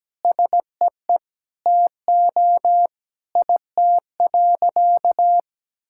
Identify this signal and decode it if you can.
Morse code, heard as a keyed tone, about 17 wpm, 700 Hz. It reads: SEE TO IT.
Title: Sentences from Top 1000 Words - Review Part 5 - 17wpm